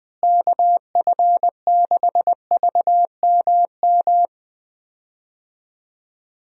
Morse code, heard as a keyed tone, about 20 wpm, 700 Hz. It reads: KF6VMM
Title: USA Call Signs - 6 Characters - Repeated 3x - 20wpm